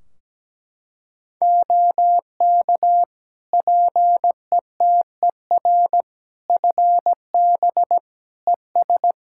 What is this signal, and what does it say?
Morse code, heard as a keyed tone, about 17 wpm, 700 Hz. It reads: OK PETER FB ES
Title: K1USN SST QSO Practice 17wpm